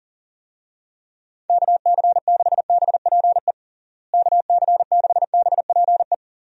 Morse code, heard as a keyed tone, about 40 wpm, 700 Hz. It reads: KC6BPE KC6BPE
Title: USA Call Signs - 6 Characters - Repeated 2x - 40wpm